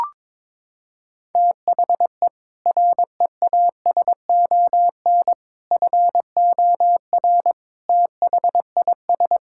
Morse code, heard as a keyed tone, about 22 wpm, 700 Hz. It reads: THE REASON FOR THIS
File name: practice / 22wpm / Sets of 7 Words - 22wpm